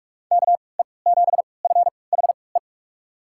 Morse code, heard as a keyed tone, about 42 wpm, spaced as for 25 wpm, 700 Hz. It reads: KE7FHE
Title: USA Call Signs - 3x Character Spacing - 45wpm